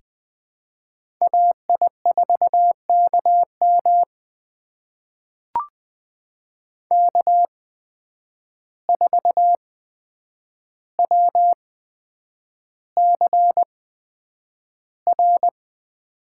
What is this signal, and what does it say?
Morse code, heard as a keyed tone, about 20 wpm, spaced as for 9 wpm, 700 Hz. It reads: AI4KM K4WCR
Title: USA Call Signs - 5 Characters - 8x Character Spacing - 20wpm